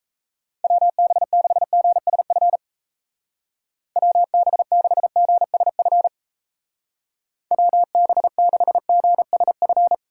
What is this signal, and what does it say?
Morse code, heard as a keyed tone, about 34 wpm, 700 Hz. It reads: WB6GSF WB6GSF WB6GSF
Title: Speed Racing - USA Call Signs 28wpm